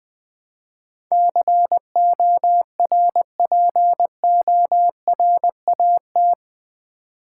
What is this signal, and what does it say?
Morse code, heard as a keyed tone, about 20 wpm, 700 Hz. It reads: CORPORAT
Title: Build A Word - 4 Letters At A Time - 11-Letter Words - 20wpm